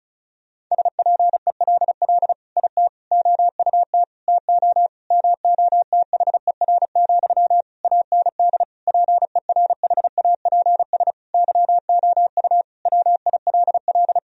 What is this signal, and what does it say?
Morse code, heard as a keyed tone, about 34 wpm, 700 Hz. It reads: SPELL IT OUT TO MOTHER, AND PERHAPS YOU WILL
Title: Sentences from Top 1000 Words - Review Part 1 - 35wpm